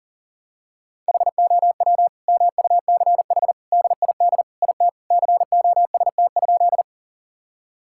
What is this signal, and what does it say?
Morse code, heard as a keyed tone, about 40 wpm, 700 Hz. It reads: HOW MUCH DID IT COST?
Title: Sentences from Top 1000 Words - Review Part 5 - 40wpm